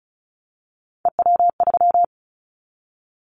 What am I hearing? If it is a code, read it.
Morse code, heard as a keyed tone, about 36 wpm, 700 Hz. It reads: EW3
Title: Callsign Prefixes for Top 50 DXCC - 35wpm